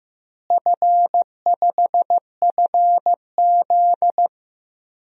Morse code, heard as a keyed tone, about 15 wpm, 700 Hz. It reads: F5FZ
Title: DXCC-100 Call Signs - 4 Characters - Repeated 2x - 15wpm